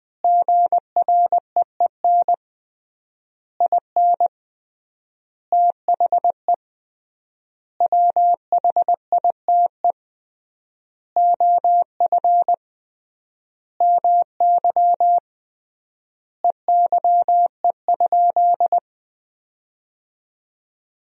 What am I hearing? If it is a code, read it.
Morse code, heard as a keyed tone, about 20 wpm, 700 Hz. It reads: GREEN IN THE WHITE OF MY EYE?
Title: 3x Word-Spacing - Sentences from Top 500 Words - 20wpm